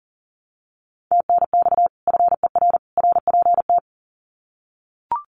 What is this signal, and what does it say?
Morse code, heard as a keyed tone, about 40 wpm, 700 Hz. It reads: TNX FER RPT
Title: Common QSO Phrases 40wpm